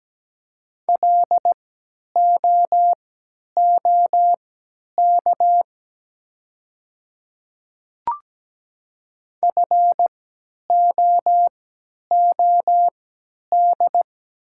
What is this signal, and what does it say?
Morse code, heard as a keyed tone, about 17 wpm, spaced as for 10 wpm, 700 Hz. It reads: LOOK FOOD
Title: 3x Character Spacing - Top 300 Words - 17wpm